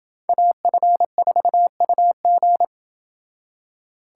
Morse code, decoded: AF4UG

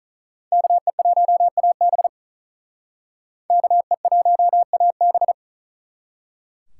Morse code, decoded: KE1AB KE1AB

